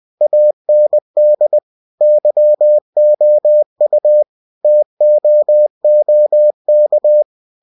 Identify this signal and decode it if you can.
Morse code, heard as a keyed tone, about 20 wpm, 600 Hz. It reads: AND YOU TOOK
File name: cw-books / winnie-the-pooh / winnie-20wpm-page-0006